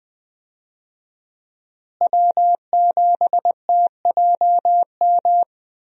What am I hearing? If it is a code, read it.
Morse code, heard as a keyed tone, about 21 wpm, 700 Hz. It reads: W7TJM
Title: USA Call Signs - 5 Characters - Repeated 3x - 20wpm